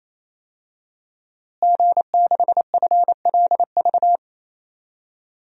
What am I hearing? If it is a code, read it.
Morse code, heard as a keyed tone, about 28 wpm, 700 Hz. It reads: G6FLV
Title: DXCC-100 Call Signs - 5 Characters - Repeated 3x - 28wpm